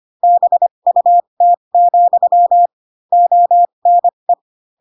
Morse code, heard as a keyed tone, about 25 wpm, spaced as for 22 wpm, 700 Hz. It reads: BUT, ONE
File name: Pinocchio / Pinocchio ch7 18 25 ik1hge